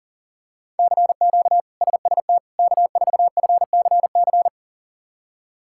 Morse code, heard as a keyed tone, about 40 wpm, 700 Hz. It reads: CQ SST K4FCC
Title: K1USN SST QSO Practice - 40wpm